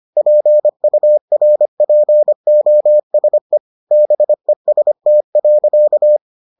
Morse code, heard as a keyed tone, about 25 wpm, 600 Hz. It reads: PURPOSE BEST.